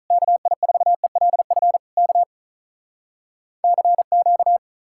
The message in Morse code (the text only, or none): KI4ELF K CQ